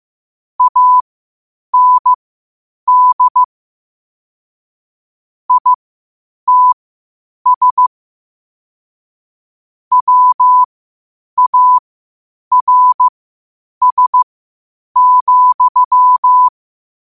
Morse code, heard as a keyed tone, about 15 wpm, spaced as for 8 wpm, 1000 Hz. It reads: AND ITS WARS,
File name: qotdfiles / 8wpm